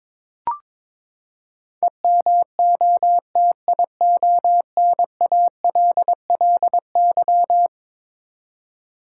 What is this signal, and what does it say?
Morse code, heard as a keyed tone, about 22 wpm, 700 Hz. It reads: EMOTIONALLY
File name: practice / 22wpm / Eleven Letter Words - 22wpm